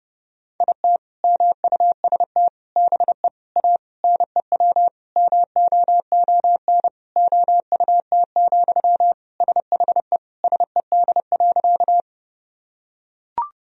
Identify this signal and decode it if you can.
Morse code, heard as a keyed tone, about 30 wpm, 700 Hz. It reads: IT MUST BE A NEW MOON OUT, SHE SED.
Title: Sentences from Top 500 Words and Abbreviations 30wpm